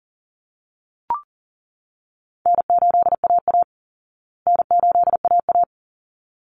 Morse code, heard as a keyed tone, about 40 wpm, 700 Hz. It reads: N8AA N8AA
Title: Easy Call Signs Repeated 2x - 40wpm